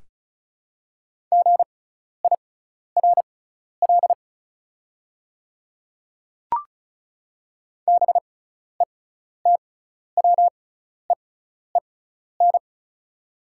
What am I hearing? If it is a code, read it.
Morse code, heard as a keyed tone, about 36 wpm, spaced as for 12 wpm, 690 Hz. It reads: GIRL BETWEEN
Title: 6x Character Spacing - Top 500 Words - 35wpm